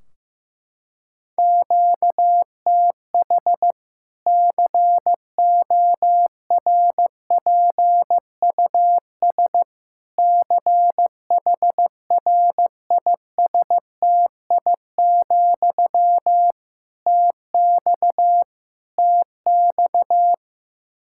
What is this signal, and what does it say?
Morse code, heard as a keyed tone, about 15 wpm, 700 Hz. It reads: QTH CORPUS CHRISTI, TX TX
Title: K1USN SST QSO Practice 15wpm